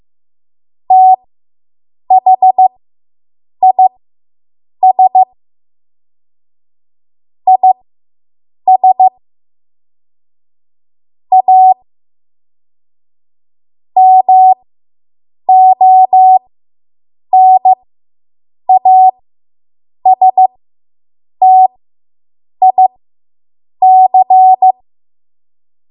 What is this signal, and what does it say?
Morse code, heard as a keyed tone, about 15 wpm, spaced as for 7 wpm, 750 Hz. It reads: THIS IS A MONASTIC